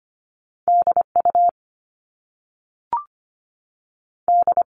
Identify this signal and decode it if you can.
Morse code, heard as a keyed tone, about 25 wpm, 700 Hz. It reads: DU D
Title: Top Two Letter Combinations in Words 25wpm